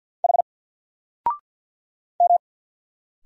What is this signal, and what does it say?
Morse code, heard as a keyed tone, about 50 wpm, 700 Hz. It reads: H M
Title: Single Letter-Number Alt 50wpm